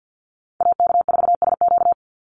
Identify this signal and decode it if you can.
Morse code, heard as a keyed tone, about 48 wpm, 700 Hz. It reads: AK4SQ
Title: Call Signs - 50wpm